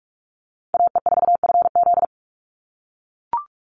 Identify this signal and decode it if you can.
Morse code, heard as a keyed tone, about 44 wpm, 700 Hz. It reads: AE4FZ
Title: NA SOTA Chaser Callsigns 45wpm